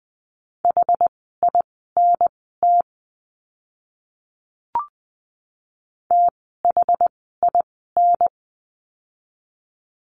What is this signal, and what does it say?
Morse code, heard as a keyed tone, about 20 wpm, spaced as for 14 wpm, 700 Hz. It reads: HINT THIN
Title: 061 Lesson 11f - Letter h - Words 20wpm